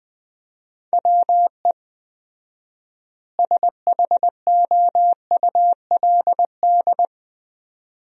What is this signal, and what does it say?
Morse code, heard as a keyed tone, about 20 wpm, 700 Hz. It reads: WE SHOULD
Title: 4x Word-Spacing - Sentences from Top 300 Words - 20wpm